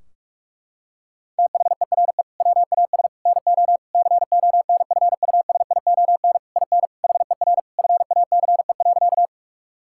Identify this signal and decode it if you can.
Morse code, heard as a keyed tone, about 43 wpm, 700 Hz. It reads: THERE WAS NO CONFUSION IN HER FACE.
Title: Sentences from Top 1600 Words 45wpm